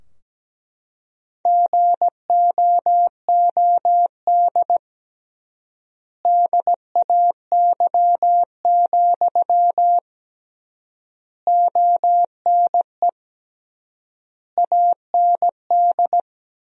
Morse code, heard as a keyed tone, about 17 wpm, 700 Hz. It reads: GOOD DAY, ONE AND
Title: 3x Word-Spacing - Sentences from Top 200 Words - 17wpm